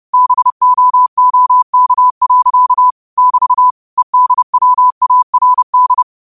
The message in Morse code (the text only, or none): BOOK. <BT> EDWARD